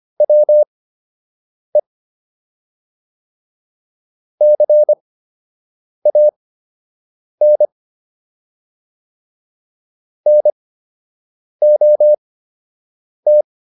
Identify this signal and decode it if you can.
Morse code, heard as a keyed tone, about 25 wpm, spaced as for 7 wpm, 600 Hz. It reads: WE CAN NOT